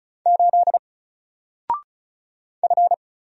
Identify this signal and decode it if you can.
Morse code, heard as a keyed tone, about 35 wpm, 700 Hz. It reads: 8 F
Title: Single Character (Letter:Number:Punctuation:Prosign) 35wpm